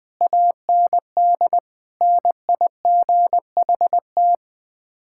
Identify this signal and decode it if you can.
Morse code, heard as a keyed tone, about 20 wpm, 700 Hz. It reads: AND NIGHT
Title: Sets of 8 Words - 20wpm